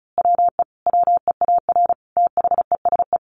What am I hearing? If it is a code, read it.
Morse code, heard as a keyed tone, about 34 wpm, 700 Hz. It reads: WE WEAR THESE
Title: A Sound Of Thunder 35wpm